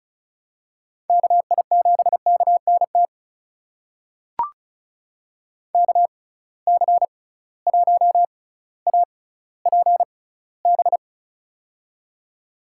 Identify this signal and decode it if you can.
Morse code, heard as a keyed tone, about 36 wpm, spaced as for 18 wpm, 700 Hz. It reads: KI7KNT KC1APB
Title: USA Call Signs - 6 Characters - 6x Character Spacing - 35wpm